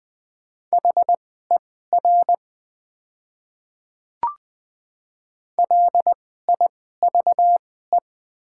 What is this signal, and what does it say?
Morse code, heard as a keyed tone, about 20 wpm, spaced as for 14 wpm, 700 Hz. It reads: HER LIVE
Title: 2x Character Spacing - Top 200 Words - 20wpm